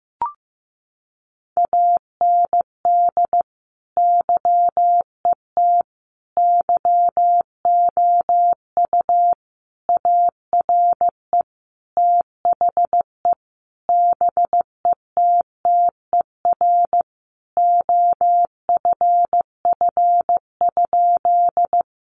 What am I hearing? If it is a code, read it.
Morse code, heard as a keyed tone, about 15 wpm, 700 Hz. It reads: AND YET YOU ARE THE BETTER OFF?
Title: Sentences from Top 500 Words 15wpm